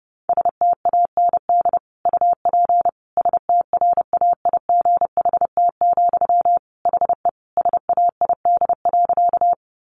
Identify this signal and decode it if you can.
Morse code, heard as a keyed tone, about 30 wpm, 700 Hz. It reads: STAND UP STRAIGHT, HE SAID.
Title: Sentences from Top 1000 Words 30wpm